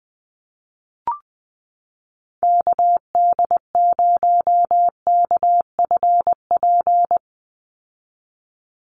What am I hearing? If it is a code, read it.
Morse code, heard as a keyed tone, about 20 wpm, 700 Hz. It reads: KD0KFP
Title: USA Call Signs 20wpm